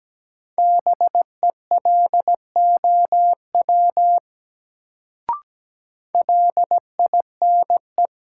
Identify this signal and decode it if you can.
Morse code, heard as a keyed tone, about 17 wpm, 700 Hz. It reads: BELOW LINE
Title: Sentences from Top 1200 Words - New Word Drill Alt - 17wpm